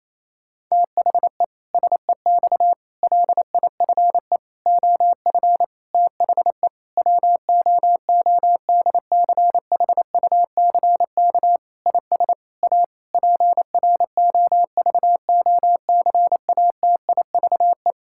Morse code, heard as a keyed tone, about 28 wpm, 700 Hz. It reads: THE SEX LIFE OF THE WOODCHUCK IS A PROVOCATIVE
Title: Pangram Challenge v2 - 28wpm